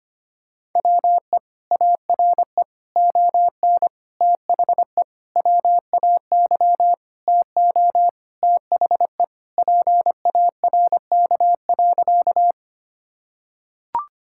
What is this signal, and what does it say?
Morse code, heard as a keyed tone, about 25 wpm, 700 Hz. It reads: WE ARE ON THE WAY TO THE PARK.
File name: practice / 25wpm / Sentences with 4-Word Phrases - 25wpm